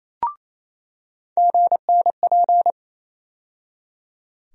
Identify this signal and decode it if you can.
Morse code, heard as a keyed tone, about 28 wpm, 700 Hz. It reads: GNP
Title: USA Call Sign Postfixes - 3 Letters 28wpm